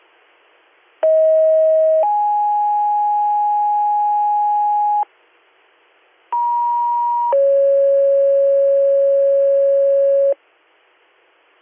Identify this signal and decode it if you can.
Two-tone page, heard as a one-second tone, then a three-second tone, about 640 then 830 Hz, about 950 then 550 Hz. It is heard from 1.0 to 10.3 s.
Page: allegany-ny